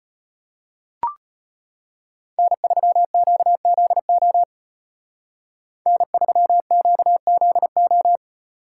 Morse code, heard as a keyed tone, about 32 wpm, 700 Hz. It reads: N3QZO N3QZO